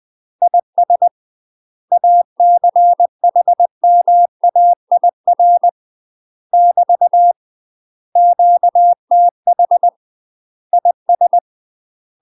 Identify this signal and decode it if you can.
Morse code, heard as a keyed tone, about 20 wpm, 700 Hz. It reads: IS ACHMAIR <BT> QTH IS
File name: UK MASTER QSOs TRAINING / 20WPM / UK MASTER QSOs 0070